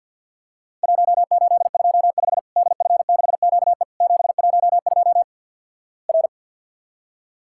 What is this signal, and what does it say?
Morse code, heard as a keyed tone, about 50 wpm, 700 Hz. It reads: 1825 DF6QE 712 R
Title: Worked All Europe DX Contest - QTC Exchange with Repeats - 50wpm